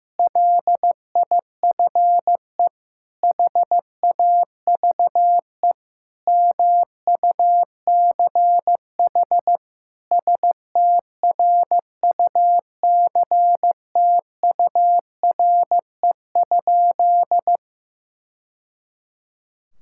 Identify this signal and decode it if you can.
Morse code, heard as a keyed tone, about 15 wpm, 700 Hz. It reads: LIFE HAVE MUCH STRUCTURE?
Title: Sentences from Top 1000 Words - Review Part 6 - 15wpm